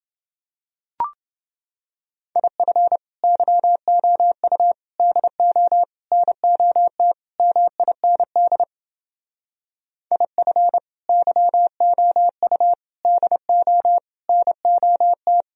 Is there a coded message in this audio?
Yes: Morse code from 2.4 to 15.4 s, heard as a keyed tone, about 29 wpm, 700 Hz. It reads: IF YOU DO NOT MIND IF YOU DO NOT